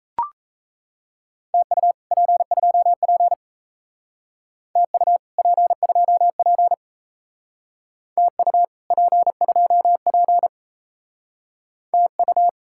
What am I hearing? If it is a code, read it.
Morse code, heard as a keyed tone, about 36 wpm, 700 Hz. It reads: TU P2P TU P2P TU P2P TU